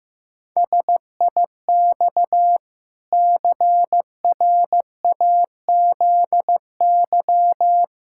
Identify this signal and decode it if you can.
Morse code, heard as a keyed tone, about 15 wpm, 700 Hz. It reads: SIX CRAZY
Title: Pangram Challenge v2 - 15wpm